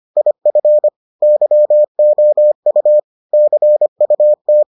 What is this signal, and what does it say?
Morse code, heard as a keyed tone, about 25 wpm, 600 Hz. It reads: IF YOU CUT